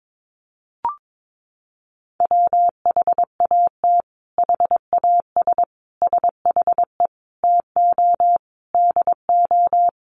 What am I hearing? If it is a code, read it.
Morse code, heard as a keyed tone, about 22 wpm, 700 Hz. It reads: WHAT HAS SHE TO DO
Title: Sentences from Top 100 Words 22wpm